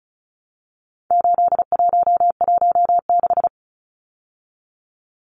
Morse code, heard as a keyed tone, about 34 wpm, 700 Hz. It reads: 8116
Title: Numbers - 4-Digit 35wpm